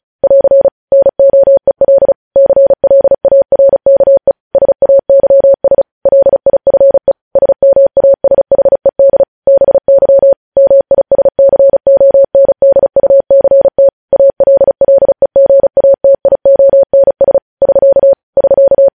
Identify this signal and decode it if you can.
Morse code, heard as a keyed tone, about 36 wpm, 550 Hz. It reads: <AR> NOEL CLARKE SAYS LIFE SMASHED BY MISCONDUCT ALLEGATIONS <SK> <SK>
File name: podcast / News headlines-1x-35x35